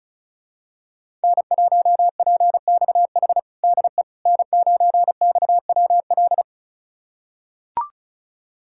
Morse code, heard as a keyed tone, about 36 wpm, 700 Hz. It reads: N1PXH DE N9XWL
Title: USA Callsign DE Callsign - 5 Characters - 35wpm